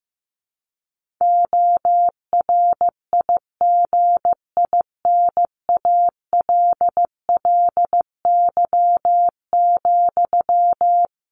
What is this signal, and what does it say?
Morse code, heard as a keyed tone, about 14 wpm, 700 Hz. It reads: ORIGINALLY,
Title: Sentences from Top 1500 Words 15wpm